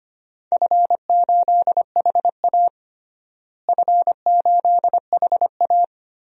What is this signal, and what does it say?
Morse code, heard as a keyed tone, about 25 wpm, 700 Hz. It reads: F8HA F8HA
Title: DXCC-100 Call Signs - 4 Characters - Repeated 2x - 25wpm